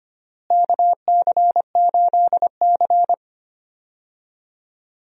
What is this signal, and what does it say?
Morse code, heard as a keyed tone, about 25 wpm, 700 Hz. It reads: KC8C
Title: USA Call Signs - 4 Characters - Repeated 3x - 25wpm